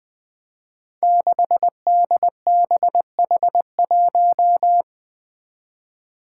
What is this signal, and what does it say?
Morse code, heard as a keyed tone, about 20 wpm, 700 Hz. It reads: 6DBH1
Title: Five Challenging Letter-Number Combinations 20wpm